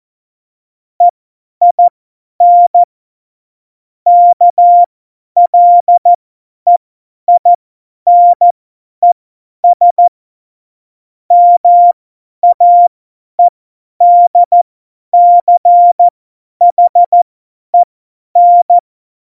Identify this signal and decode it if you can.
Morse code, heard as a keyed tone, about 14 wpm, spaced as for 10 wpm, 700 Hz. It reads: EIN KLEINES MAEDCHEN
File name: Morsekurs / Witze / Witz 22. Januar 2019, 50 F70